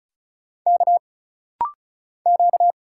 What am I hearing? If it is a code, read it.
Morse code, heard as a keyed tone, about 35 wpm, 700 Hz. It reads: K Q
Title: Single Letter-Number Alt - Rapid-Fire 35wpm